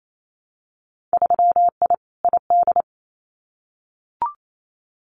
Morse code, heard as a keyed tone, about 28 wpm, 700 Hz. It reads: 3I ID
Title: Winter Field Day Exchanges 28wpm